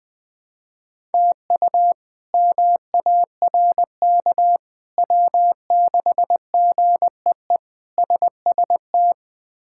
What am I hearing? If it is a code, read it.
Morse code, heard as a keyed tone, about 20 wpm, 700 Hz. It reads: TU MARK W6GEE SST
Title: K1USN SST QSO Practice - 20wpm